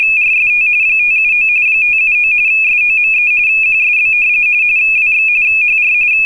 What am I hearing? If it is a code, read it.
RTTY, about 45.45 baud. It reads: RST 579 579 SOME QSB WX HR